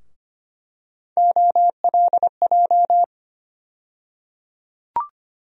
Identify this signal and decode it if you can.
Morse code, heard as a keyed tone, about 25 wpm, 700 Hz. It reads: OLJ
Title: USA Call Sign Postfixes - 3 Letters 25wpm